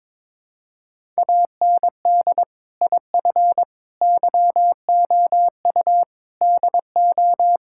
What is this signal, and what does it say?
Morse code, heard as a keyed tone, about 22 wpm, 700 Hz. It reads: AND IF YOU DO
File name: practice / 22wpm / Build A Phrase - 5-Word Phrases - 22wpm